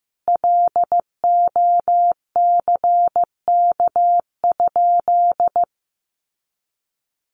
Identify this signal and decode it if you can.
Morse code, heard as a keyed tone, about 15 wpm, 700 Hz. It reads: LOCK?
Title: Sentences from Top 1300 Words 15wpm